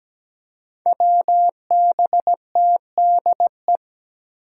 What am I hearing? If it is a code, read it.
Morse code, heard as a keyed tone, about 17 wpm, 700 Hz. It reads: WBTDE
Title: Five Challenging Letter-Number Combinations 17wpm